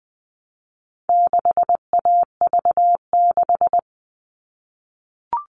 Morse code, heard as a keyed tone, about 20 wpm, 700 Hz. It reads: 6AV6
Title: All Cut Numbers - 4-Digit 20wpm